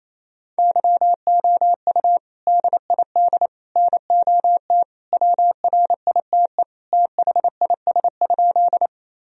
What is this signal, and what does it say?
Morse code, heard as a keyed tone, about 27 wpm, 700 Hz. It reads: YOU DID NOT WRITE THIS?
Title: Sentences from Top 1000 Words - Review Part 1 - 28wpm